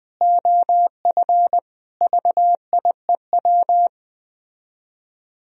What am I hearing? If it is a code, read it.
Morse code, heard as a keyed tone, about 20 wpm, 700 Hz. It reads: OF VIEW